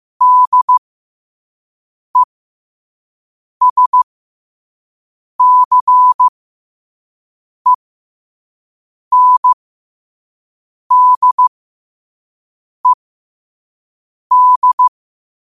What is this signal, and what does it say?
Morse code, heard as a keyed tone, about 15 wpm, spaced as for 5 wpm, 1000 Hz. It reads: DESCENDED
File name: qotdfiles / 5wpm